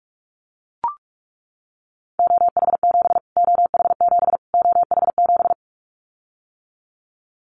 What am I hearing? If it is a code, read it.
Morse code, heard as a keyed tone, about 44 wpm, 700 Hz. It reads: OH7 OH7 OH7
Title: Callsign Prefixes for Top 25 DXCC - Repeated 3x - 45wpm